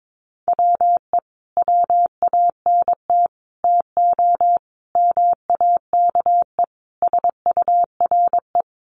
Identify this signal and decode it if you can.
Morse code, heard as a keyed tone, about 22 wpm, 700 Hz. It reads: WE WANT TO MAKE SURE
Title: Sets of 5 Words Encore2 22wpm